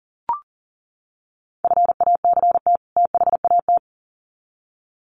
Morse code, heard as a keyed tone, about 40 wpm, 700 Hz. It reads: FACT THAT
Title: Sets of 2 Words 40wpm